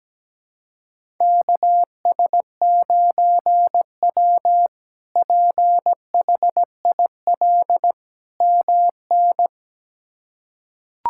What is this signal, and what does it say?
Morse code, heard as a keyed tone, about 17 wpm, 700 Hz. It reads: KS9W PHIL MN
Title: K1USN SST QSO Practice - 17wpm